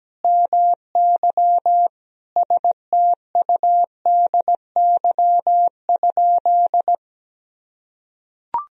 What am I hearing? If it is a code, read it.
Morse code, heard as a keyed tone, about 17 wpm, 700 Hz. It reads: MY STUDY?
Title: Sentences from Top 1000 Words - Review Part 1 - 17wpm